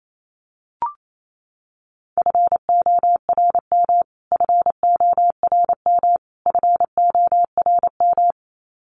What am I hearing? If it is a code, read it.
Morse code, heard as a keyed tone, about 28 wpm, 700 Hz. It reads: FORM FORM FORM
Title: Top 500 Words Repeated 28wpm